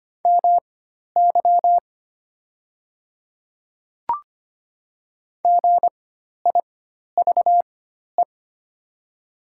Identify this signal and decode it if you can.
Morse code, heard as a keyed tone, about 25 wpm, spaced as for 12 wpm, 700 Hz. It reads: MY GIVE
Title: 4x Character Spacing - Top 500 Words - 25wpm